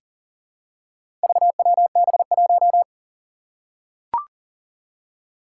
Morse code, heard as a keyed tone, about 38 wpm, 700 Hz. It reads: VWB1
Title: Four Challenging Letter-Number Combinations 40wpm